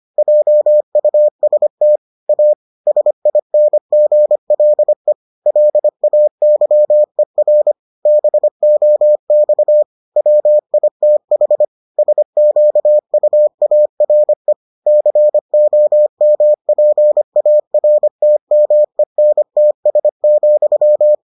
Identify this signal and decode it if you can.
Morse code, heard as a keyed tone, about 25 wpm, 600 Hz. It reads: JUST A SINGLE LAYER BOX WITH SQUARE COMPARTMENTS,